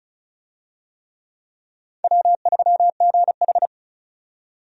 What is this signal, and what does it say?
Morse code, heard as a keyed tone, about 34 wpm, 700 Hz. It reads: W3GH
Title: USA Call Signs - 4 Characters - 35wpm